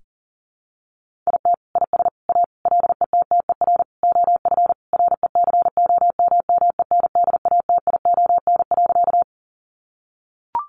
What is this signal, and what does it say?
Morse code, heard as a keyed tone, about 40 wpm, 700 Hz. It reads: IT IS A LETTER OF RECOMMENDATION.
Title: Sentences from Top 1400 Words 40wpm